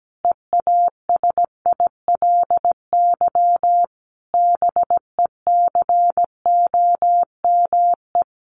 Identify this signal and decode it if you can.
Morse code, heard as a keyed tone, about 17 wpm, 700 Hz. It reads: EASILY BECOME
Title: Sentences from Top 900 Words 17wpm